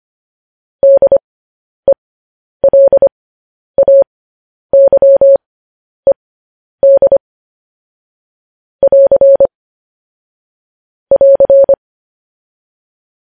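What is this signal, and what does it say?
Morse code, heard as a keyed tone, about 25 wpm, spaced as for 10 wpm, 550 Hz. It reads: DELAYED <AR> <AR>
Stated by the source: News headlines sent in Morse code at 10 wpm.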